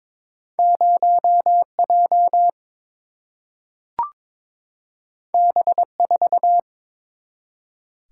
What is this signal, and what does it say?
Morse code, heard as a keyed tone, about 22 wpm, 700 Hz. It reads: 0 J B 4
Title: Single-Letter Single-Number 22wpm